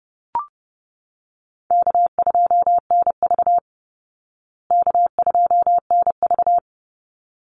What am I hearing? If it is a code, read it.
Morse code, heard as a keyed tone, about 30 wpm, 700 Hz. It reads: K2NV K2NV
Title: Call Signs Repeated 3x - 30wpm